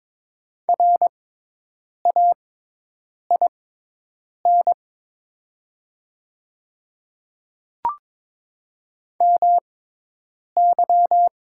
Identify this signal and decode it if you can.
Morse code, heard as a keyed tone, about 22 wpm, spaced as for 8 wpm, 700 Hz. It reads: RAIN MY
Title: 6x Character Spacing - Top 300 Words - 22wpm